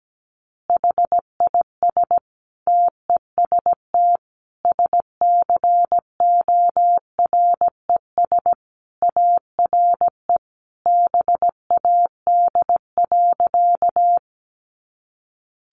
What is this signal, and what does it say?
Morse code, heard as a keyed tone, about 17 wpm, 700 Hz. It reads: HIS TEST SCORES ARE BAD.
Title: Additional Easy Phrases 17wpm